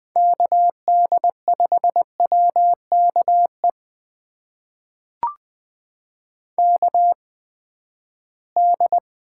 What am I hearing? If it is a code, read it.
Morse code, heard as a keyed tone, about 20 wpm, spaced as for 14 wpm, 700 Hz. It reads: KD5WKE KD
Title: USA Call Signs - 6 Characters - 8x Character Spacing - 20wpm